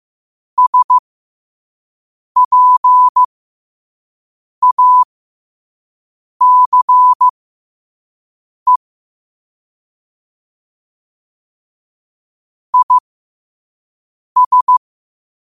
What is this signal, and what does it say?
Morse code, heard as a keyed tone, about 15 wpm, spaced as for 5 wpm, 1000 Hz. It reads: SPACE IS
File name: qotdfiles / 5wpm